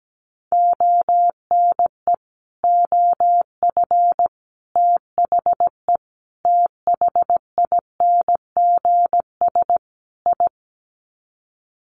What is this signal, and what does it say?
Morse code, heard as a keyed tone, about 17 wpm, 700 Hz. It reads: ONE OF THE THINGS I
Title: Sets of 5 Words 17wpm